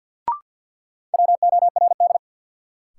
Morse code, heard as a keyed tone, about 48 wpm, 700 Hz. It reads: WORD
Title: Sentences from Top 300 Words - New Word Drill - 50wpm